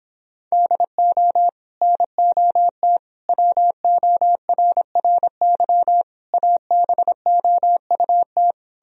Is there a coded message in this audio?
Yes: Morse code from 0.5 to 8.5 s, heard as a keyed tone, about 29 wpm, 700 Hz. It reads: DO NOT WORRY ABOUT